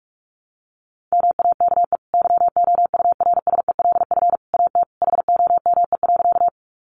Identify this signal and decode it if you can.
Morse code, heard as a keyed tone, about 44 wpm, 700 Hz. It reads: MAKE YOURSELF AT HOME.
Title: Sentences from Top 600 Words 45wpm